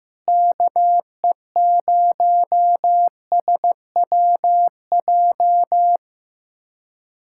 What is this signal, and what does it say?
Morse code, heard as a keyed tone, about 15 wpm, 700 Hz. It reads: KE0SWJ
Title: USA Callsign DE Callsign - 6 Characters - 15wpm